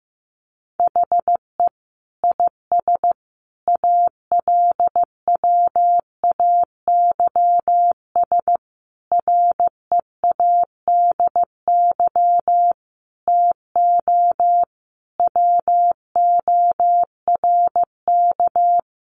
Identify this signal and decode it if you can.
Morse code, heard as a keyed tone, about 15 wpm, 700 Hz. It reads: HE IS ALWAYS READY TO WORK
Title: Sentences from Top 1300 Words 15wpm